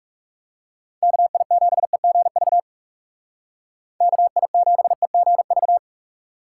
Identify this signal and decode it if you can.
Morse code, heard as a keyed tone, about 37 wpm, 700 Hz. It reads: KI7EGV KI7EGV
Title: Speed Racing - USA 2x3 Call Signs 30wpm